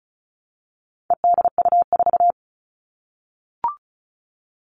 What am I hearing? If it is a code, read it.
Morse code, heard as a keyed tone, about 34 wpm, 700 Hz. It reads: EDU4
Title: All Cut Numbers - 4-Digit 35wpm